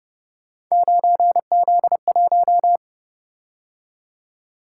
Morse code, heard as a keyed tone, about 29 wpm, 700 Hz. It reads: 9Z1